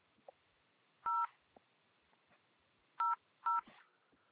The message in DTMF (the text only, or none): 0 00